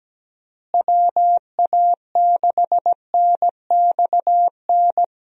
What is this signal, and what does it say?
Morse code, heard as a keyed tone, about 17 wpm, 700 Hz. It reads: WA6NXN
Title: USA Call Signs - 6 Characters - Repeated 2x - 17wpm